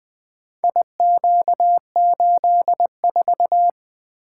Morse code, heard as a keyed tone, about 23 wpm, 700 Hz. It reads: IQ84
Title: Speed-Racing - Four Letter-Number Combinations - 20wpm